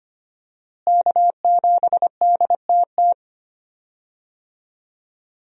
Morse code, heard as a keyed tone, about 25 wpm, 700 Hz. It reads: K7DTT
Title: USA Call Signs - 5 Characters - Repeated 3x - 25wpm